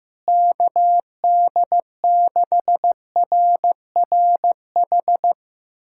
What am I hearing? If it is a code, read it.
Morse code, heard as a keyed tone, about 16 wpm, 700 Hz. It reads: KD6RRH
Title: USA Callsign DE Callsign 15wpm